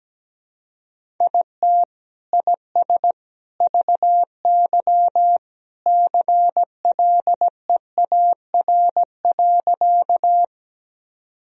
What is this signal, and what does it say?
Morse code, heard as a keyed tone, about 17 wpm, 700 Hz. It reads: IT IS VY CLEAR.
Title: Sentences from Top 500 Words and Abbreviations 17wpm